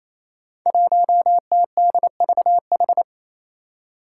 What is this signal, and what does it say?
Morse code, heard as a keyed tone, about 27 wpm, 700 Hz. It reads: 1TDVH
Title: Five Challenging Letter-Number Combinations 28wpm